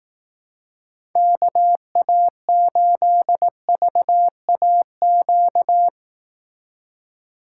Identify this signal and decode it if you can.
Morse code, heard as a keyed tone, about 18 wpm, 700 Hz. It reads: KA8VAQ
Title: Speed Racing - USA 2x3 Call Signs 15wpm